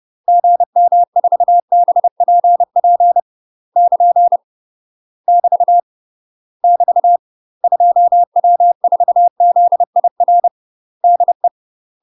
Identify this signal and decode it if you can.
Morse code, heard as a keyed tone, about 30 wpm, 700 Hz. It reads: GM4BPP <KN> <BT> <BT> 2W4ZIR DE
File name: UK MASTER QSOs TRAINING / 30WPM / UK MASTER QSOs 0011